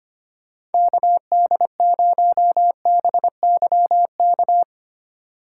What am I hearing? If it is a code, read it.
Morse code, heard as a keyed tone, about 28 wpm, 700 Hz. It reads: KD0BYK
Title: Speed Racing - USA 2x3 Call Signs 25wpm